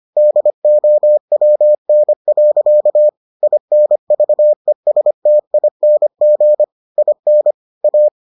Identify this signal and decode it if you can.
Morse code, heard as a keyed tone, about 25 wpm, 600 Hz. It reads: DOWN. INVESTING IN A